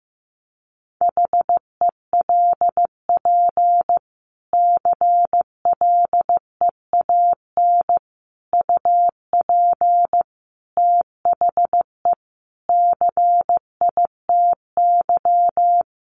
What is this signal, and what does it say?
Morse code, heard as a keyed tone, about 15 wpm, 700 Hz. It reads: HELP CLEAN UP THE CITY
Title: Sentences from Top 1100 Words 15wpm